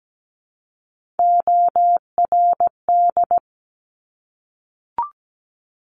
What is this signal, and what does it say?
Morse code, heard as a keyed tone, about 17 wpm, 700 Hz. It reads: ORD
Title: Top Three Letter Combinations in Words 17wpm